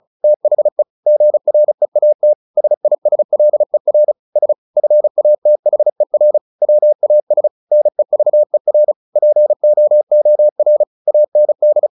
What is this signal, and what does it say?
Morse code, heard as a keyed tone, about 34 wpm, 600 Hz. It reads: THE GREAT SISLER S FATHER WAS NEVER POOR AND
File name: the old man and the sea / the old man and the sea 1 35wpm